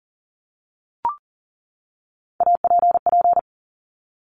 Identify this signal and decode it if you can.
Morse code, heard as a keyed tone, about 40 wpm, 700 Hz. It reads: APP
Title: Top Three Letter Combinations in Words 40wpm